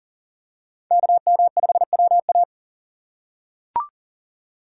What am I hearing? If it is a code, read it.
Morse code, heard as a keyed tone, about 40 wpm, 700 Hz. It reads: KM5WA
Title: USA Call Signs - 5 Characters - 2x Character Spacing - 40wpm